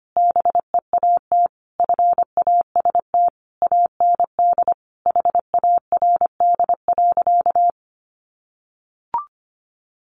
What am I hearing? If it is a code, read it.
Morse code, heard as a keyed tone, about 25 wpm, 700 Hz. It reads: BEAT FAST AND HARD.
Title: Sentences from Top 1300 Words 25wpm